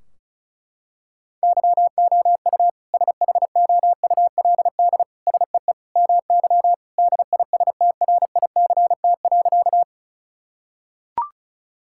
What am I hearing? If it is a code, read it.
Morse code, heard as a keyed tone, about 36 wpm, 700 Hz. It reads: YOU SHOULD SEE MY DISTRICT.
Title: Sentences from Top 1600 Words 35wpm